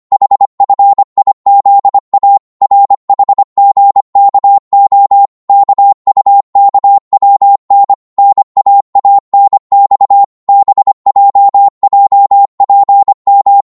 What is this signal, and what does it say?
Morse code, heard as a keyed tone, about 25 wpm, 800 Hz. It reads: HFIZA RHGKO KUKWN NAANX BJJPM
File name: Morse / ltr-25wpm